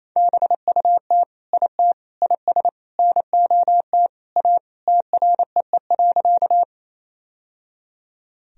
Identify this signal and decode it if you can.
Morse code, heard as a keyed tone, about 28 wpm, 700 Hz. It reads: BUT IT IS NOT A TREE.
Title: Sentences from Top 1000 Words - Review Part 1 - 28wpm